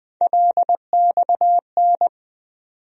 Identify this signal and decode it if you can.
Morse code, heard as a keyed tone, about 20 wpm, 700 Hz. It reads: LXN
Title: USA Call Sign Postfixes - 3 Letters 20wpm